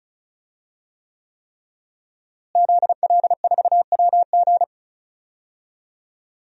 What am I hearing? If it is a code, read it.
Morse code, heard as a keyed tone, about 35 wpm, 700 Hz. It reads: ZL4WG
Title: DXCC-100 Call Signs - 5 Characters - Repeated 3x - 35wpm